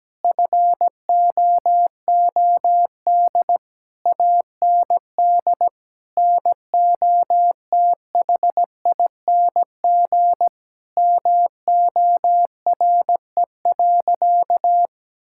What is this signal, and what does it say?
Morse code, heard as a keyed tone, about 17 wpm, 700 Hz. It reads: FOOD AND NOTHING MORE.
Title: Sentences from Top 1000 Words - Review Part 5 - 17wpm